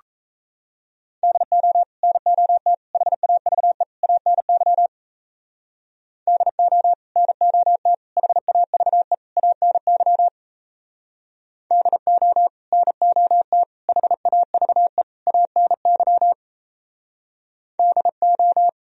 Morse code, heard as a keyed tone, about 33 wpm, 700 Hz. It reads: DO NOT HAVE ANY DO NOT HAVE ANY DO NOT HAVE ANY DO
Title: Speed-Racing - Sets of 4 Words 28wpm